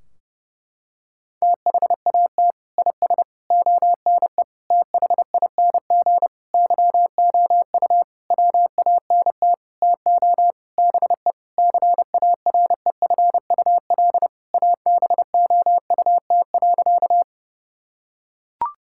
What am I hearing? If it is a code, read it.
Morse code, heard as a keyed tone, about 32 wpm, 700 Hz. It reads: THAT IS ONE THING YOU WANT TO BE CAREFUL ABOUT.
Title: Sentences from Top 1600 Words 30wpm